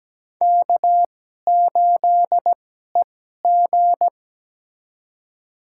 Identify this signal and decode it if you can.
Morse code, heard as a keyed tone, about 17 wpm, spaced as for 12 wpm, 700 Hz. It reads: K8EG
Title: USA Call Signs - 4 Characters - 2x Character Spacing - 17wpm